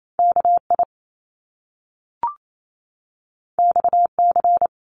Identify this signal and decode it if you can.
Morse code, heard as a keyed tone, about 28 wpm, 700 Hz. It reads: K I X C
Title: Top Two Letter Combinations in Words 28wpm